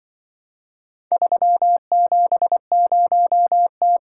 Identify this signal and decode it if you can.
Morse code, heard as a keyed tone, about 25 wpm, 700 Hz. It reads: 370T
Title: Speed-Racing - Four Letter-Number Combinations - 20wpm